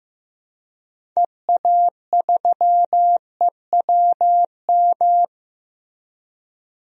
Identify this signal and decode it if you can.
Morse code, heard as a keyed tone, about 15 wpm, 700 Hz. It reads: EA3EWM
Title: DXCC-100 Call Signs - 6 Characters - Repeated 3x - 15wpm